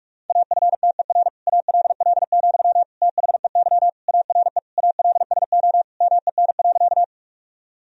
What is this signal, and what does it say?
Morse code, heard as a keyed tone, about 46 wpm, 700 Hz. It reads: AFTER ALL, THEY ARE ALSO MEN.
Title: Sentences from Top 1000 Words - Review Part 1 - 45wpm